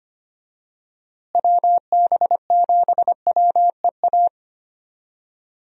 Morse code, heard as a keyed tone, about 25 wpm, 700 Hz. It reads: WB7WEA
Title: USA Call Signs - 6 Characters - Repeated 2x - 25wpm